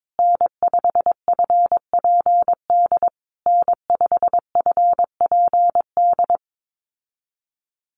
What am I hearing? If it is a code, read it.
Morse code, heard as a keyed tone, about 22 wpm, 700 Hz. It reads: N5FPD N5FPD
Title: USA Call Signs Repeated 3x 22wpm